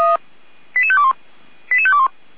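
Five-tone selective call (ZVEI), heard as quick runs of five tones, tones of about 70 ms. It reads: 88521 88521
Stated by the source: ZVEI five-tone call received by SDR.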